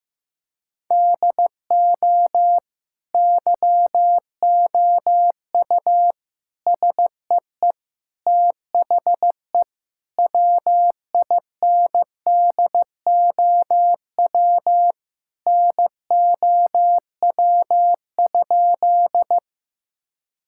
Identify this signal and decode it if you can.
Morse code, heard as a keyed tone, about 15 wpm, 700 Hz. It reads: DO YOU SEE THE WINDOW NOW?
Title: Sentences from Top 1600 Words 15wpm